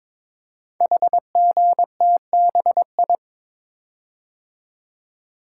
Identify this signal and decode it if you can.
Morse code, heard as a keyed tone, about 22 wpm, 690 Hz. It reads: HGTBI
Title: Code Groups - 5 Letters - 22wpm